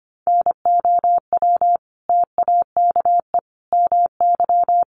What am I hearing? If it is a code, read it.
Morse code, heard as a keyed tone, about 24 wpm, 700 Hz. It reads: NOW TAKE MY
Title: Sentences from Top 1100 Words 25wpm